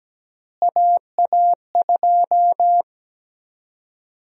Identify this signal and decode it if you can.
Morse code, heard as a keyed tone, about 17 wpm, 700 Hz. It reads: AA2
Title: USA Call Sign Prefixes - Repeated 2x - 17wpm